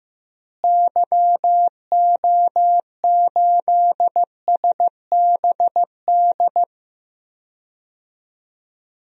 Morse code, heard as a keyed tone, about 15 wpm, 700 Hz. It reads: YO8SBD
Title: DXCC-100 Call Signs - 6 Characters - Repeated 3x - 15wpm